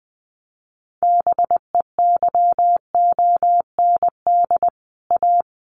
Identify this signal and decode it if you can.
Morse code, heard as a keyed tone, about 20 wpm, 700 Hz. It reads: BEYOND A
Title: Sentences from Top 1500 Words 20wpm